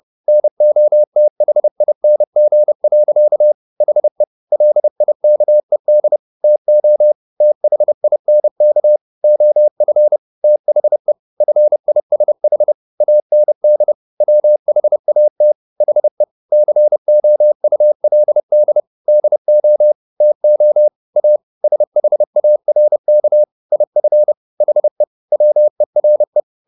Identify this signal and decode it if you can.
Morse code, heard as a keyed tone, about 30 wpm, 600 Hz. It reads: NOTHING. HE LIKED TO THINK OF THE FISH AND WHAT HE COULD DO TO A SHARK IF HE WERE